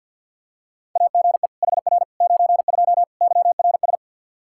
Morse code, heard as a keyed tone, about 46 wpm, 700 Hz. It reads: AGE HR 92 YRS
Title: Practice QSO - Exchange 3 - More Info - 50wpm